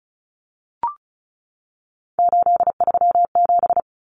Morse code, heard as a keyed tone, about 34 wpm, 700 Hz. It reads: 837
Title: Numbers - 3-Digit 35wpm